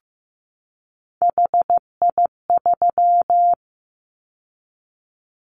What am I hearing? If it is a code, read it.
Morse code, heard as a keyed tone, about 15 wpm, 700 Hz. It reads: HI3
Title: Callsign Prefixes for Top 100 DXCC - 15wpm